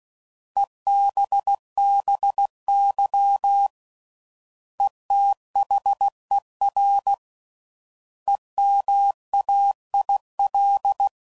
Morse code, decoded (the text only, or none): EBBY ETHER EMAIL